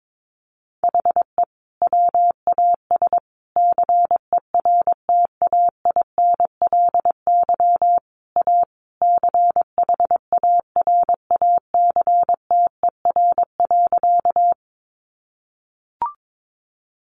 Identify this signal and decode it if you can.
Morse code, heard as a keyed tone, about 22 wpm, 700 Hz. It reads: HE WAS CERTAINLY A CHARACTER.
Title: Sentences from Top 1200 Words 22wpm